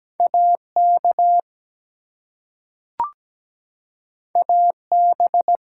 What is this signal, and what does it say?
Morse code, heard as a keyed tone, about 17 wpm, 700 Hz. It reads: AK AB
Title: US States and Canadian Provinces 17wpm